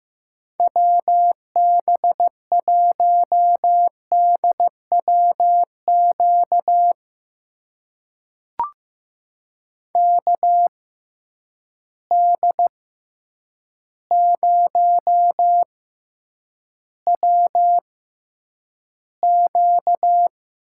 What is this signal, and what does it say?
Morse code, heard as a keyed tone, about 15 wpm, spaced as for 8 wpm, 700 Hz. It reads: WB1DWQ KD0WQ
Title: USA Call Signs - 6 Characters - 6x Character Spacing - 15wpm